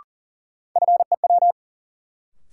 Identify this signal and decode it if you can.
Morse code, heard as a keyed tone, about 40 wpm, 700 Hz. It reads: FEW
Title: Sets of 2 Words - New Word Drill - 40wpm